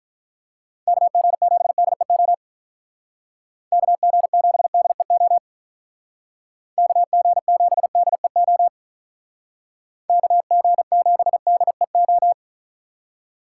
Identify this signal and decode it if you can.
Morse code, heard as a keyed tone, about 46 wpm, 700 Hz. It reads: KG7DEO KG7DEO KG7DEO KG7DEO